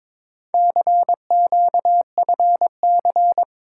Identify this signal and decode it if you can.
Morse code, heard as a keyed tone, about 22 wpm, 700 Hz. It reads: CQFC
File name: practice / 22wpm / Four Letter-Number Combinations - 22wpm